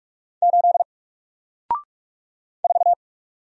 Morse code, heard as a keyed tone, about 45 wpm, 700 Hz. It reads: 8 4
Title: Single Letter-Number Alt 45wpm